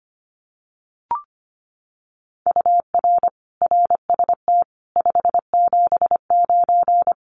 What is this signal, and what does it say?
Morse code, heard as a keyed tone, about 25 wpm, 700 Hz. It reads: UR RST 579
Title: Common QSO Phrases Encore 25wpm